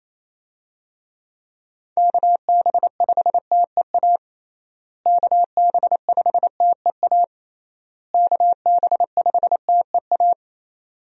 Morse code, heard as a keyed tone, about 28 wpm, 700 Hz. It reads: KB5TEA KB5TEA KB5TEA
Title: USA Call Signs - 6 Characters - Repeated 3x - 28wpm